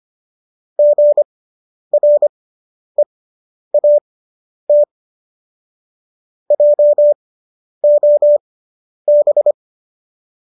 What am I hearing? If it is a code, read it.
Morse code, heard as a keyed tone, about 25 wpm, spaced as for 10 wpm, 590 Hz. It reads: GREAT JOB